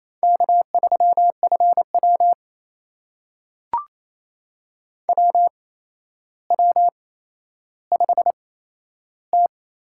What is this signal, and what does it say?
Morse code, heard as a keyed tone, about 28 wpm, spaced as for 12 wpm, 700 Hz. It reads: K3FW WW5T
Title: USA Call Signs - 4 Characters - 8x Character Spacing - 28wpm